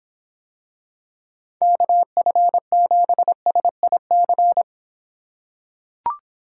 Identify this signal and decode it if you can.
Morse code, heard as a keyed tone, about 26 wpm, 700 Hz. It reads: KF7SIC